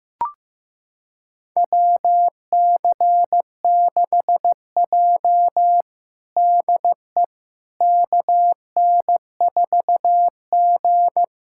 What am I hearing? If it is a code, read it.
Morse code, heard as a keyed tone, about 15 wpm, 700 Hz. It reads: WC6J DE KN4G
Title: USA Callsign DE Callsign - 4 Characters - 15wpm